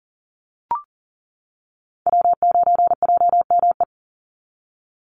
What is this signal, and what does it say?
Morse code, heard as a keyed tone, about 40 wpm, 700 Hz. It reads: W9JME
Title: USA Call Signs 40wpm